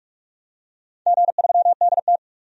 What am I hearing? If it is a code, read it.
Morse code, heard as a keyed tone, about 44 wpm, 700 Hz. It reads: G3DT
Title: DXCC-100 Call Signs - 4 Characters - 45wpm